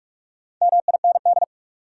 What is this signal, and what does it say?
Morse code, heard as a keyed tone, about 46 wpm, 700 Hz. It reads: MIND
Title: Sentences from Top 500 Words - New Word Drill - 45wpm